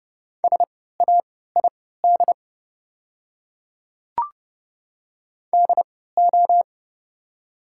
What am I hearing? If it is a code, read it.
Morse code, heard as a keyed tone, about 30 wpm, spaced as for 17 wpm, 690 Hz. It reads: SAID DO
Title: 3x Character Spacing - Top 500 Words - 30wpm